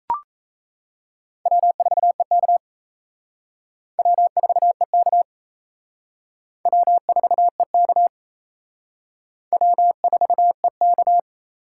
Morse code, heard as a keyed tone, about 35 wpm, 700 Hz. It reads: W4EK W4EK W4EK W4EK